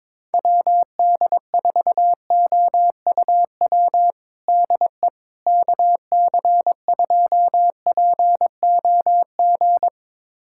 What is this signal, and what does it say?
Morse code, heard as a keyed tone, about 22 wpm, 700 Hz. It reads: WD4OUW DE KC2POG
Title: USA Callsign DE Callsign 22wpm